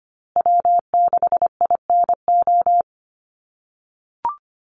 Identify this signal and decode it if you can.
Morse code, heard as a keyed tone, about 25 wpm, 700 Hz. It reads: W6INO
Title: USA Call Signs Repeated 2x 25wpm